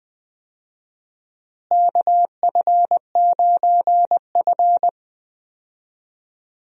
Morse code, heard as a keyed tone, about 20 wpm, 700 Hz. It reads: KF9F
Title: USA Call Signs - 4 Characters - Repeated 2x - 20wpm